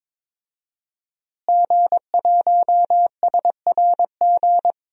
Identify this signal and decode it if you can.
Morse code, heard as a keyed tone, about 22 wpm, 700 Hz. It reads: G1SRG
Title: DXCC-100 Call Signs - 5 Characters - Repeated 2x - 22wpm